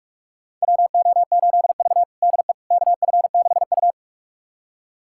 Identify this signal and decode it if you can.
Morse code, heard as a keyed tone, about 45 wpm, 700 Hz. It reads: WO8V DE KF6U
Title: USA Callsign DE Callsign - 4 Characters - 45wpm